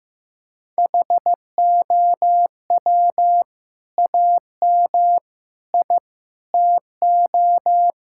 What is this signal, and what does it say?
Morse code, heard as a keyed tone, about 14 wpm, 700 Hz. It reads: HOW AM I TO
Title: Sentences from Top 1000 Words - Review Part 5 - 15wpm